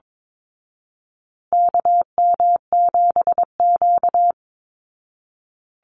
Morse code, heard as a keyed tone, about 22 wpm, 700 Hz. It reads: KM7Q
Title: Call Signs Repeated 2x - 22wpm